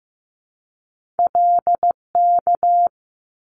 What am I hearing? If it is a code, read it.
Morse code, heard as a keyed tone, about 15 wpm, 700 Hz. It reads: LK
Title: Top Two Letter Combinations in Words 15wpm